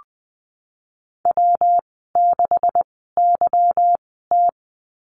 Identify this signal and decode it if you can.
Morse code, heard as a keyed tone, about 20 wpm, spaced as for 15 wpm, 700 Hz. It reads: W6YT
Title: 154 Lesson 25g - Letter p - Call Signs 20wpm